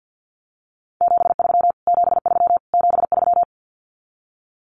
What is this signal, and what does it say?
Morse code, heard as a keyed tone, about 49 wpm, 700 Hz. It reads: 73 73 73
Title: Abbreviations Repeated 3x - 50wpm